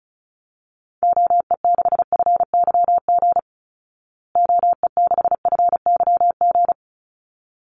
Morse code, heard as a keyed tone, about 34 wpm, 700 Hz. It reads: OE6FYG OE6FYG
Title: Call Signs Repeated 3x - 35wpm